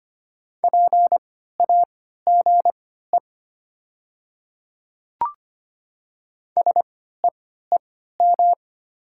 Morse code, heard as a keyed tone, about 25 wpm, spaced as for 14 wpm, 690 Hz. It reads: PAGE SEEM
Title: 3x Character Spacing - Top 300 Words - 25wpm